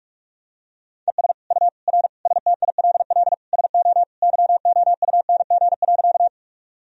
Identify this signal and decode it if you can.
Morse code, heard as a keyed tone, about 43 wpm, 700 Hz. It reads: ES U R STILL SO YOUNG.
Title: Sentences from Top 400 Words and Abbreviations 45wpm